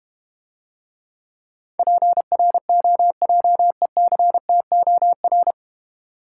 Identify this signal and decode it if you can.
Morse code, heard as a keyed tone, about 32 wpm, 700 Hz. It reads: PROJECTOR